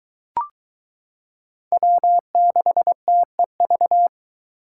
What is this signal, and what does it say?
Morse code, heard as a keyed tone, about 22 wpm, 700 Hz. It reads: W6TEV